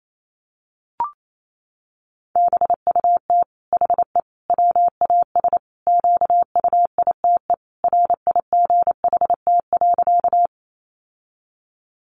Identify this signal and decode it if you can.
Morse code, heard as a keyed tone, about 28 wpm, 700 Hz. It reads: BUT HE WAS QUITE RIGHT.
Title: Sentences from Top 700 Words 28wpm